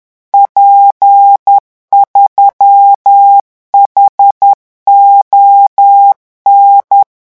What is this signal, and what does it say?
Morse code, heard as a keyed tone, about 10 wpm, 800 Hz. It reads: P3HON
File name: morse mp3 / revision13-vitesse 10 mots